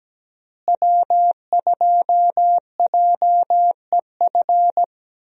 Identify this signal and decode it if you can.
Morse code, heard as a keyed tone, about 17 wpm, 700 Hz. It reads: W2JEF
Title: USA Call Signs - 5 Characters - Repeated 2x - 17wpm